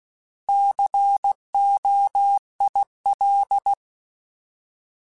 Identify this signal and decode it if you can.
Morse code, heard as a keyed tone, about 16 wpm, 800 Hz. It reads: COIL
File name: Alphabetical / Alphabetical mp3s / Cs @15